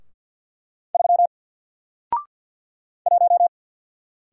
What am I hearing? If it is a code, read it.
Morse code, heard as a keyed tone, about 48 wpm, 700 Hz. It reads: 3 1
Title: Single Letter-Number Alt 50wpm